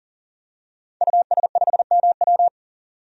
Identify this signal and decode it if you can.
Morse code, heard as a keyed tone, about 40 wpm, 700 Hz. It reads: US5MW
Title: DXCC-100 Call Signs - 5 Characters - 40wpm